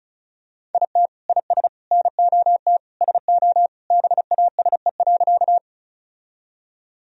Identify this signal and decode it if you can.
Morse code, heard as a keyed tone, about 34 wpm, 700 Hz. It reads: IT IS NOT SO BASE.
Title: Sentences from Top 1000 Words - Review Part 1 - 35wpm